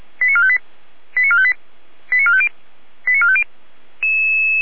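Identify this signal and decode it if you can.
Five-tone selective call (ZVEI), heard as quick runs of five tones, tones of about 70 ms. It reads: 89458 89458 89450 89450
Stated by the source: ZVEI five-tone selective call 88289, received on an SDR.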